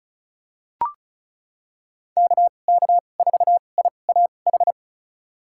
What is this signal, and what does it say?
Morse code, heard as a keyed tone, about 34 wpm, spaced as for 25 wpm, 700 Hz. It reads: KK4IAH
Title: USA Call Signs - 6 Characters - 2x Character Spacing - 35wpm